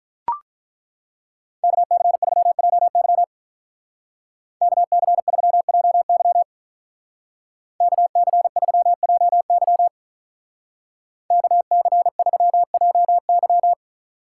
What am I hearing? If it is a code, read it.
Morse code, heard as a keyed tone, about 41 wpm, 700 Hz. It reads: KC3JY KC3JY KC3JY KC3JY
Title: Speed Racing - USA 2x2 Call Signs 35wpm